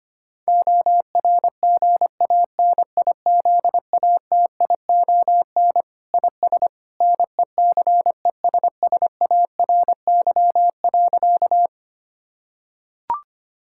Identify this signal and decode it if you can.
Morse code, heard as a keyed tone, about 25 wpm, 700 Hz. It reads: ORGANIZATION IS NECESSARY.
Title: Sentences from Top 1000 Words - Review Part 5 - 25wpm